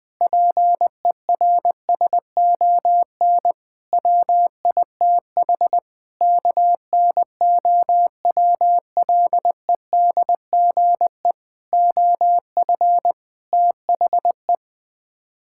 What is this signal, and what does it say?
Morse code, heard as a keyed tone, about 20 wpm, 700 Hz. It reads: PERSON WITH KNOWLEDGE OF THE
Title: Sets of 8 Words - 20wpm